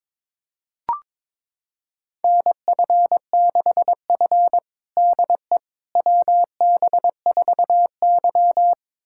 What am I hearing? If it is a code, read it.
Morse code, heard as a keyed tone, about 22 wpm, 700 Hz. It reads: NF6F DE WB4Y
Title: USA Callsign DE Callsign - 4 Characters - 22wpm